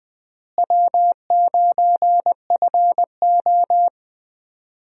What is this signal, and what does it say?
Morse code, heard as a keyed tone, about 21 wpm, 700 Hz. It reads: W9FO
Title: USA Call Signs - 4 Characters - Repeated 3x - 20wpm